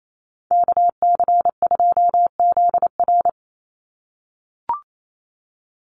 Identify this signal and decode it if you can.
Morse code, heard as a keyed tone, about 28 wpm, 700 Hz. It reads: KC2ZR
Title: USA Call Signs 28wpm